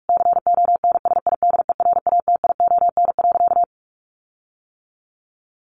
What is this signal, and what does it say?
Morse code, heard as a keyed tone, about 44 wpm, 700 Hz. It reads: CONSIDERATION.
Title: Sentences from Top 1500 Words 45wpm